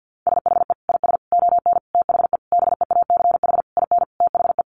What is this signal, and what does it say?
Morse code, heard as a keyed tone, about 50 wpm, 700 Hz. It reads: SHE IS ON THE BEACH IN THE
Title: Sentences from Top 1500 Words 50wpm